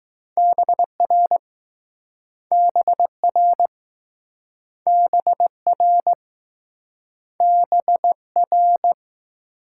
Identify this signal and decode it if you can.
Morse code, heard as a keyed tone, about 20 wpm, 700 Hz. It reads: BR BR BR BR